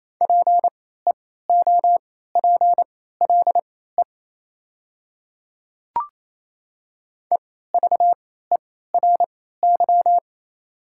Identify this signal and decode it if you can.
Morse code, heard as a keyed tone, about 28 wpm, spaced as for 16 wpm, 700 Hz. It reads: PEOPLE EVERY
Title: 3x Character Spacing - Top 300 Words - 28wpm